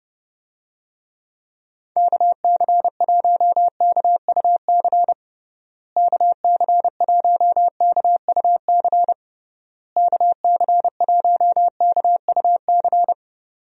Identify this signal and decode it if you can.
Morse code, heard as a keyed tone, about 30 wpm, 700 Hz. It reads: KC1KUC KC1KUC KC1KUC
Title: USA Call Signs - 6 Characters - Repeated 3x - 30wpm